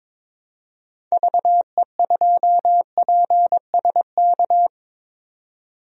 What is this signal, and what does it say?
Morse code, heard as a keyed tone, about 22 wpm, 700 Hz. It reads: VE2PSK
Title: DXCC-100 Call Signs - 6 Characters - 22wpm